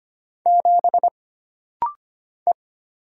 Morse code, heard as a keyed tone, about 25 wpm, 700 Hz. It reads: 7 E
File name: Single Character (Letter:Number:Punctuation:Prosign) - Rapid-Fire - 25wpm